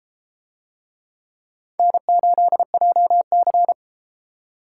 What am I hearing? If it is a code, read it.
Morse code, heard as a keyed tone, about 30 wpm, 700 Hz. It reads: N8JC